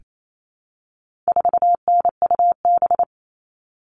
Morse code, heard as a keyed tone, about 28 wpm, 700 Hz. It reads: 4NUB
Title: All Cut Numbers - 4-Digit 28wpm